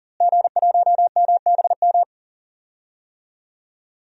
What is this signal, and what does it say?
Morse code, heard as a keyed tone, about 40 wpm, 690 Hz. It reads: G1MBM
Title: DXCC-100 Call Signs - 5 Characters - Repeated 2x - 40wpm